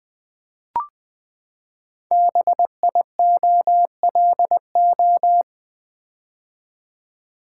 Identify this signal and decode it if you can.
Morse code, heard as a keyed tone, about 20 wpm, 700 Hz. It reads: BIOLO
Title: Build A Word - 5 Letters At A Time - 10-Letter Words - 20wpm